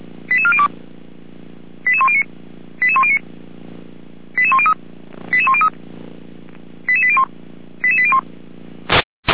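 Five-tone selective call (ZVEI), heard as quick runs of five tones, tones of about 80 ms. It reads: 88402 88109 88109 88103 88103 88801 88801
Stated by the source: SDR-received ZVEI five-tone selective call 88192, recorded by feuerwehr-orlamuende (fire brigade Orlamünde).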